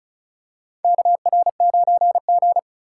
Morse code, heard as a keyed tone, about 36 wpm, 700 Hz. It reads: KR9G
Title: USA Call Signs - 3x Character Spacing - 35wpm